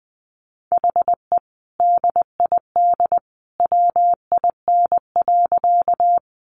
Morse code, heard as a keyed tone, about 20 wpm, 700 Hz. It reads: HE DID WIN.
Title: Sentences from Top 1500 Words 20wpm